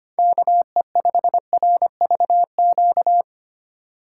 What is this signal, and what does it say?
Morse code, heard as a keyed tone, about 25 wpm, 700 Hz. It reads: KE5RVQ
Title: USA Call Signs - 6 Characters - 4x Character Spacing - 25wpm